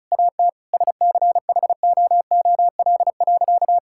Morse code, heard as a keyed tone, about 34 wpm, 700 Hz. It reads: AT SCHOOL.
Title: Sentences from Top 1000 Words - Review Part 6 - 35wpm